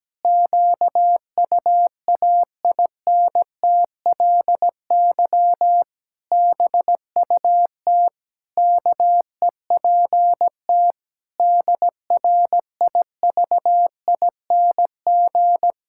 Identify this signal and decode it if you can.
Morse code, heard as a keyed tone, about 17 wpm, 700 Hz. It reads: QUAINTLY BUT KEPT DRIVING
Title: Pangram Challenge v2 - 17wpm